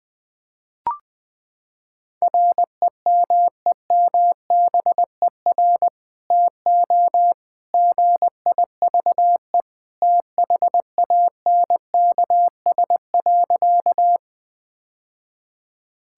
Morse code, heard as a keyed tone, about 20 wpm, 700 Hz. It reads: REMEMBER TO GIVE THANKS.
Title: Sentences with 2-Word Phrases - 20wpm